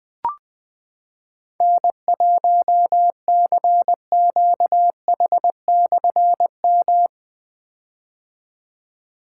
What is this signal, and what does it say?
Morse code, heard as a keyed tone, about 20 wpm, 700 Hz. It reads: N1CQH/M
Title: Difficult Call Signs 20wpm